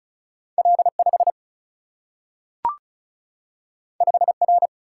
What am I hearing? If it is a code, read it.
Morse code, heard as a keyed tone, about 36 wpm, 700 Hz. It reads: L 5 5 R
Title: Single-Letter Single-Number 35wpm